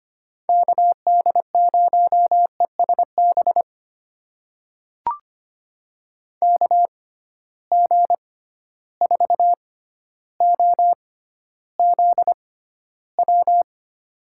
Morse code, heard as a keyed tone, about 25 wpm, spaced as for 13 wpm, 700 Hz. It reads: KD0ESB KG4OZW
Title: USA Call Signs - 6 Characters - 6x Character Spacing - 25wpm